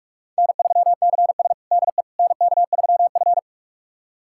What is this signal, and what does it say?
Morse code, heard as a keyed tone, about 42 wpm, 700 Hz. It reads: N3CS DE NK3F
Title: USA Callsign DE Callsign - 4 Characters - 45wpm